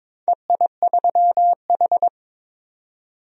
Morse code, heard as a keyed tone, about 22 wpm, 690 Hz. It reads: EI3H
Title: DXCC-100 Call Signs - 4 Characters - 22wpm